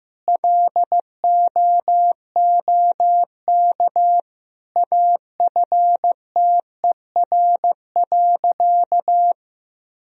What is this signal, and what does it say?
Morse code, heard as a keyed tone, about 15 wpm, 700 Hz. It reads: LOOK AFTER.
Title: Sentences from Top 1000 Words - Review Part 3 - 15wpm